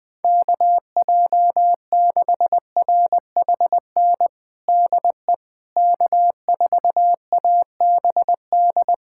Morse code, heard as a keyed tone, about 21 wpm, 700 Hz. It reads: KJ6RHN DE K4ABD
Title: USA Callsign DE Callsign 20wpm